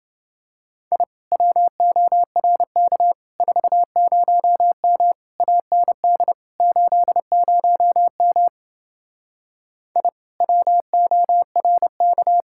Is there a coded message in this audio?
Yes: Morse code from 0.9 to 12.4 s, heard as a keyed tone, about 28 wpm, 700 Hz. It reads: I WORK 40M AND 80M I WORK